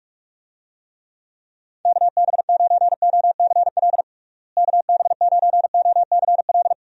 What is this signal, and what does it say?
Morse code, heard as a keyed tone, about 46 wpm, 700 Hz. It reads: KB9OCL KB9OCL
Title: USA Call Signs - 6 Characters - Repeated 3x - 45wpm